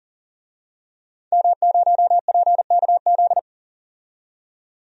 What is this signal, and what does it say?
Morse code, heard as a keyed tone, about 40 wpm, 700 Hz. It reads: M0PKZ
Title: DXCC-100 Call Signs - 5 Characters - 40wpm